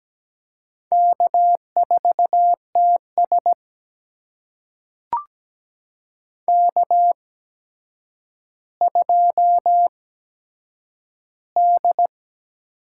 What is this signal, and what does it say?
Morse code, heard as a keyed tone, about 17 wpm, spaced as for 8 wpm, 700 Hz. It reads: K4TS K2D
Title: USA Call Signs - 4 Characters - 8x Character Spacing - 17wpm